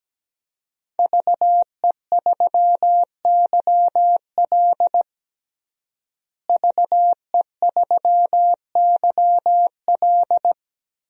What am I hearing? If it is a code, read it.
Morse code, heard as a keyed tone, about 17 wpm, 700 Hz. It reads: VE3YL VE3YL
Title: DXCC-100 Call Signs - 5 Characters - Repeated 3x - 17wpm